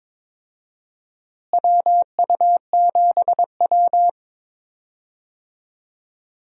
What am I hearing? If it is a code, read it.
Morse code, heard as a keyed tone, about 22 wpm, 700 Hz. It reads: WU7W
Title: USA Call Signs - 4 Characters - Repeated 2x - 22wpm